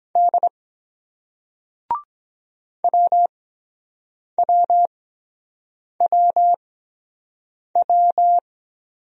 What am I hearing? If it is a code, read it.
Morse code, heard as a keyed tone, about 23 wpm, spaced as for 7 wpm, 700 Hz. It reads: D WWWW